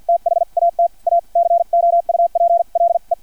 Morse code, heard as a keyed tone, about 50 wpm, spaced as for 41 wpm, 680 Hz. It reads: THAT A COUPLE
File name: archived / CRSnet